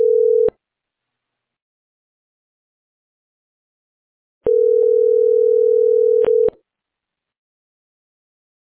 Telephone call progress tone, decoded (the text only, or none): ringback tone